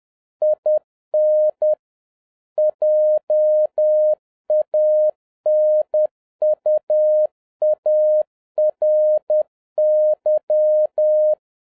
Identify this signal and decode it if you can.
Morse code, heard as a keyed tone, about 10 wpm, 610 Hz. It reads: IN JANUARY
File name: 2011-07-mp3 / n41 EXCITING QRP CW OPERATION 10wpm